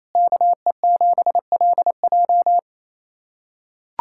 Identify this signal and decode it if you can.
Morse code, heard as a keyed tone, about 28 wpm, 700 Hz. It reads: KE7LJ
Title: USA Callsign DE Callsign - 5 Characters - 28wpm